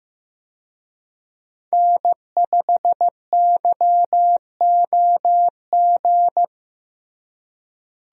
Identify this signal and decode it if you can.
Morse code, heard as a keyed tone, about 15 wpm, 700 Hz. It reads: N5YOG
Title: USA Call Signs - 5 Characters - Repeated 2x - 15wpm